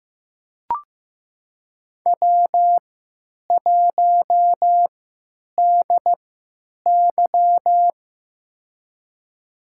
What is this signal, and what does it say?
Morse code, heard as a keyed tone, about 15 wpm, spaced as for 9 wpm, 700 Hz. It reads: W1DY
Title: USA Call Signs - 4 Characters - 3x Character Spacing - 15wpm